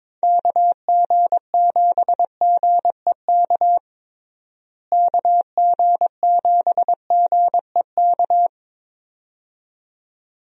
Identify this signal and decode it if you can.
Morse code, heard as a keyed tone, about 22 wpm, 700 Hz. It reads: KG7GEK KG7GEK
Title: USA Call Signs - 6 Characters - Repeated 2x - 22wpm